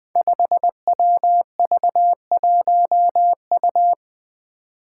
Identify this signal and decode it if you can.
Morse code, heard as a keyed tone, about 20 wpm, 700 Hz. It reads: 5WV1U
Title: Five Challenging Letter-Number Combinations 20wpm